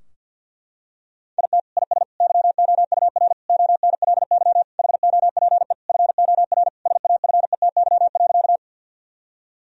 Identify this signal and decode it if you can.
Morse code, heard as a keyed tone, about 50 wpm, 700 Hz. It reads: IT IS YOUR ONLY HOPE FOR SAFETY.
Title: Sentences from Top 1000 Words - Review Part 4 - 50wpm